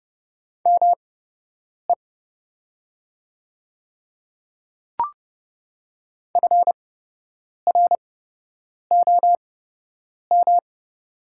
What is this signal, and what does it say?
Morse code, heard as a keyed tone, about 30 wpm, spaced as for 8 wpm, 690 Hz. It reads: ME FROM
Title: 8x Character Spacing - Top 200 Words - 30wpm